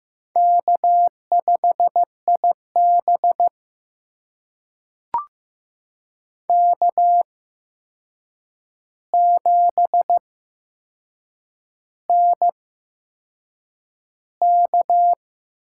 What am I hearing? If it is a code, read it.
Morse code, heard as a keyed tone, about 15 wpm, spaced as for 6 wpm, 700 Hz. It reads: K5IB K7NK
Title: USA Call Signs - 4 Characters - 8x Character Spacing - 15wpm